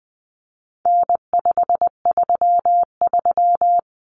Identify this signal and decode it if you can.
Morse code, heard as a keyed tone, about 20 wpm, 700 Hz. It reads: N533
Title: Common Cut Numbers - 4-Digit 20wpm